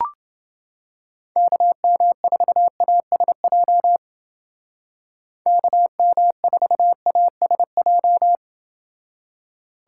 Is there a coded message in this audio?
Yes: Morse code from 1.4 to 8.4 s, heard as a keyed tone, about 28 wpm, 700 Hz. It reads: KM4ASJ KM4ASJ